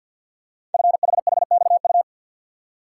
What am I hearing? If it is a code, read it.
Morse code, heard as a keyed tone, about 50 wpm, 700 Hz. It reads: FHHXU
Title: Code Groups - 5 Letters - 50wpm